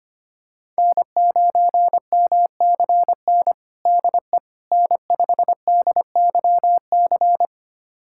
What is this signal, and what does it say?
Morse code, heard as a keyed tone, about 25 wpm, 700 Hz. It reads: N9MCN DE N5DYC
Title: USA Callsign DE Callsign - 5 Characters - 25wpm